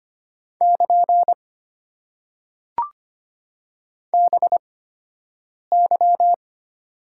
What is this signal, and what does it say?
Morse code, heard as a keyed tone, about 25 wpm, spaced as for 7 wpm, 690 Hz. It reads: <KN> BY
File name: Top 90 QSO Elements - 8x Character Spacing - 25wpm